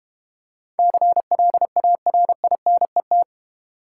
Morse code, heard as a keyed tone, about 31 wpm, 700 Hz. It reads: CLARINET
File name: Easy Words - Speed-Racing with Hints - 25wpm